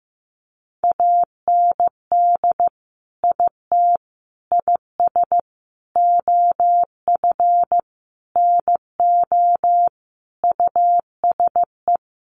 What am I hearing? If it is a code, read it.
Morse code, heard as a keyed tone, about 15 wpm, 700 Hz. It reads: AND IT IS OF NO USE
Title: Sentences from Top 1100 Words 15wpm